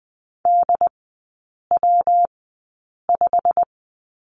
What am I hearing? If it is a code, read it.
Morse code, heard as a keyed tone, about 20 wpm, spaced as for 8 wpm, 690 Hz. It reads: DW5
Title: 145 Lesson 24e - Letter y - Three Characters 20wpm